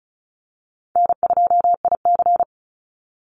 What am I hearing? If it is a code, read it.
Morse code, heard as a keyed tone, about 34 wpm, 700 Hz. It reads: N2IC
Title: Call Signs Repeated 3x - 35wpm